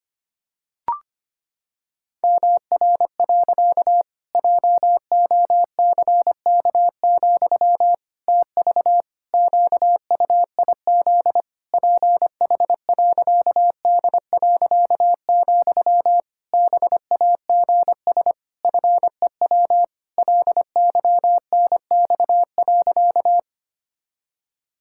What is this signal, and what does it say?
Morse code, heard as a keyed tone, about 25 wpm, 700 Hz. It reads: MR. JOCK, TV QUIZ PH.D., BAGS FEW LYNX.
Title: Pangram Challenge v2 - 25wpm